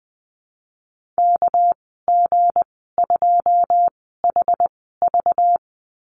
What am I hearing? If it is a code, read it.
Morse code, heard as a keyed tone, about 20 wpm, spaced as for 14 wpm, 700 Hz. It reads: KG2HV
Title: 219 Lesson 35g - Number 0 - Call Signs 20wpm